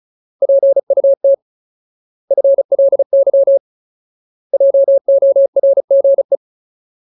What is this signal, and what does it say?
Morse code, heard as a keyed tone, about 37 wpm, 550 Hz. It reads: PUT FLY JORGE
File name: Ultimate Word List / 35WPM Ultimate